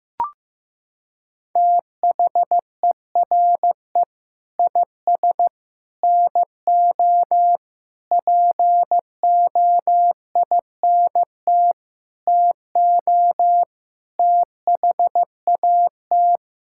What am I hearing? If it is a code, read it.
Morse code, heard as a keyed tone, about 15 wpm, 700 Hz. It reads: THERE IS NO POINT TO THAT
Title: Sentences from Top 1000 Words - Review Part 5 - 15wpm